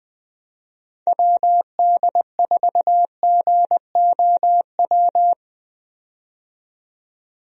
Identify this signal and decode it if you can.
Morse code, heard as a keyed tone, about 21 wpm, 700 Hz. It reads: WD4GOW
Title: USA Call Signs - 6 Characters - 20wpm